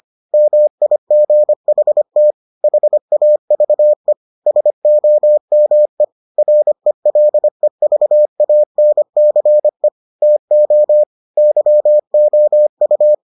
Morse code, heard as a keyed tone, about 25 wpm, 600 Hz. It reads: MIGHT HAVE SOME RELEVANCE TO YOU